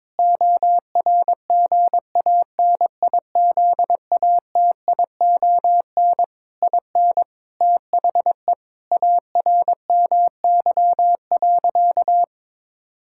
Morse code, heard as a keyed tone, about 21 wpm, 700 Hz. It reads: ORGANIZATION IN THE ARMY.
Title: Sentences from Top 1000 Words - Review Part 5 - 22wpm